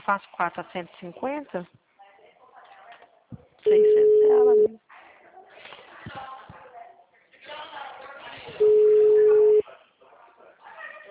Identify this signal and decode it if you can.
Ringback tone (Europe, 425 Hz), with speech.
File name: out- -2003- - - .2282